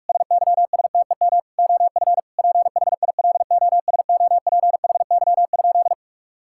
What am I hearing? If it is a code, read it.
Morse code, heard as a keyed tone, about 41 wpm, 700 Hz. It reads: SYSTEM OF PHILOSOPHY?
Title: Sentences from Top 1000 Words - Review Part 6 - 45wpm